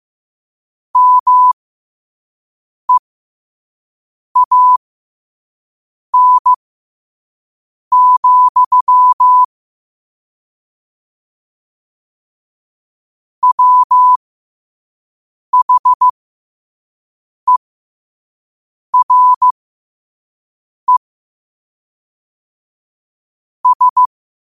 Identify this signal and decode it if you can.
Morse code, heard as a keyed tone, about 15 wpm, spaced as for 5 wpm, 1000 Hz. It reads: MEAN, WHERE S
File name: qotdfiles / 5wpm